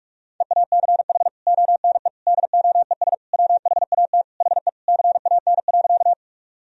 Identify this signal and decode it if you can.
Morse code, heard as a keyed tone, about 46 wpm, 700 Hz. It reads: EACH ONE DOES WHAT HE CAN.
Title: Sentences from Top 1000 Words - Review Part 1 - 45wpm